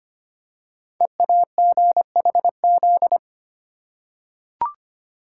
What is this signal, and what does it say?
Morse code, heard as a keyed tone, about 25 wpm, 700 Hz. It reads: EAGHZ
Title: Code Groups - 5 Letters - 25wpm